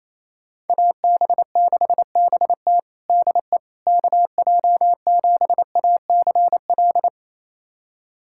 Morse code, heard as a keyed tone, about 28 wpm, 700 Hz. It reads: AB6BT DE KJ7ACL
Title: USA Callsign DE Callsign 28wpm